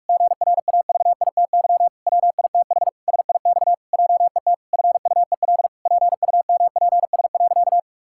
Morse code, heard as a keyed tone, about 46 wpm, 700 Hz. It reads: GRAVITY WITH SIX JET FUEL PUMPS.
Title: Pangram Challenge v2 - 45wpm